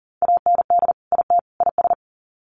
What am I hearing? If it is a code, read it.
Morse code, heard as a keyed tone, about 40 wpm, 700 Hz. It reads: AND IT IS
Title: Sets of 3 Words 40wpm